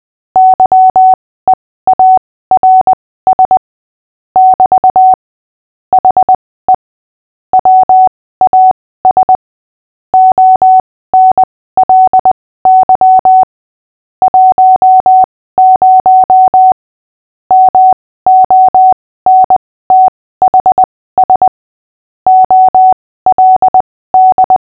Morse code, heard as a keyed tone, about 20 wpm, spaced as for 15 wpm, 750 Hz. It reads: YEARS <BT> HE WAS ONLY 10 MONTHS OLD
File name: cw / fists / Short-Story-5-15wpm